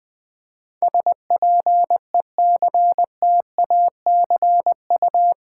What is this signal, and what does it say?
Morse code, heard as a keyed tone, about 20 wpm, 700 Hz. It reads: SPECTACU
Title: Build A Word - 4 Letters At A Time - 11-Letter Words - 20wpm